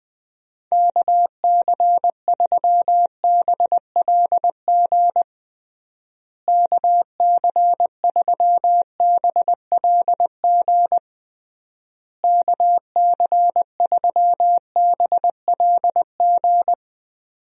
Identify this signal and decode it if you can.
Morse code, heard as a keyed tone, about 20 wpm, 700 Hz. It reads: KC3BLG KC3BLG KC3BLG
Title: USA Call Signs - 6 Characters - Repeated 3x - 20wpm